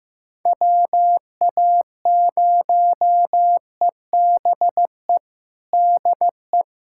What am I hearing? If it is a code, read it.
Morse code, heard as a keyed tone, about 15 wpm, 700 Hz. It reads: WA0EBE DE
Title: USA Callsign DE Callsign - 6 Characters - 15wpm